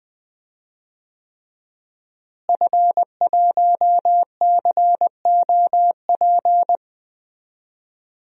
Morse code, heard as a keyed tone, about 20 wpm, 700 Hz. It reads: F1COP
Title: DXCC-100 Call Signs - 5 Characters - Repeated 3x - 20wpm